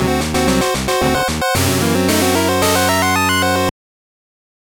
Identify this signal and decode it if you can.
There is no code.